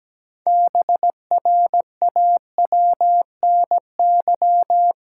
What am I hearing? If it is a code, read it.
Morse code, heard as a keyed tone, about 17 wpm, 700 Hz. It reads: BRAWNY
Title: Pangram Challenge v2 - 17wpm